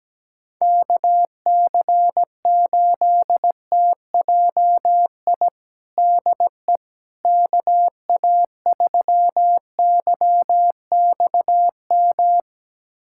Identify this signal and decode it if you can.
Morse code, heard as a keyed tone, about 17 wpm, 700 Hz. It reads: KC8TJI DE KA3YXM
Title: USA Callsign DE Callsign - 6 Characters - 17wpm